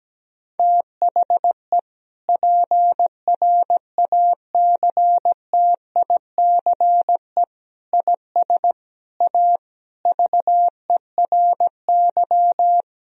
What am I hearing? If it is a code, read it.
Morse code, heard as a keyed tone, about 17 wpm, 700 Hz. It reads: THE PRACTICE IS A VERY
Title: Sentences from Top 1000 Words - Review Part 4 - 17wpm